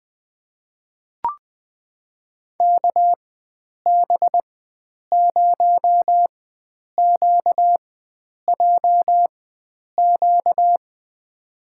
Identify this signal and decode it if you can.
Morse code, heard as a keyed tone, about 20 wpm, spaced as for 9 wpm, 700 Hz. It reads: KB0QJQ
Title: USA Call Signs - 6 Characters - 4x Character Spacing - 20wpm